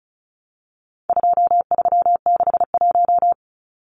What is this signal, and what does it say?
Morse code, heard as a keyed tone, about 34 wpm, 700 Hz. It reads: 2361
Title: Numbers - 4-Digit 35wpm